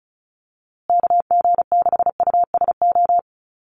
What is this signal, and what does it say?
Morse code, heard as a keyed tone, about 34 wpm, 700 Hz. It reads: KG6USO
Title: USA Call Signs Repeated 3x 35wpm